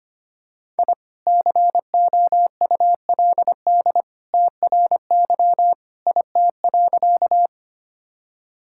Morse code, heard as a keyed tone, about 25 wpm, 700 Hz. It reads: I COULD TRY IT.
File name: practice / 25wpm / Sentences with 2-Word Phrases - 25wpm